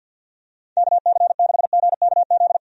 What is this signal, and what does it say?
Morse code, heard as a keyed tone, about 52 wpm, 700 Hz. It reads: KC6GKZ
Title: USA Call Signs - 6 Characters - 50wpm